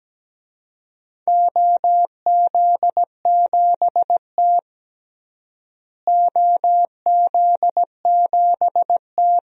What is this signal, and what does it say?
Morse code, heard as a keyed tone, about 17 wpm, 700 Hz. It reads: OZ7T OZ7T
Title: DXCC-100 Call Signs - 4 Characters - Repeated 2x - 17wpm